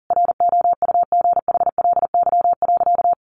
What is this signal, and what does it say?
Morse code, heard as a keyed tone, about 40 wpm, 700 Hz. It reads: ROUGHLY.
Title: Sentences from Top 1500 Words 40wpm